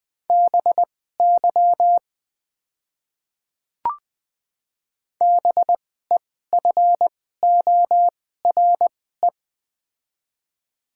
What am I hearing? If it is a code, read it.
Morse code, heard as a keyed tone, about 20 wpm, spaced as for 14 wpm, 700 Hz. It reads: BY BEFORE
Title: 2x Character Spacing - Top 300 Words - 20wpm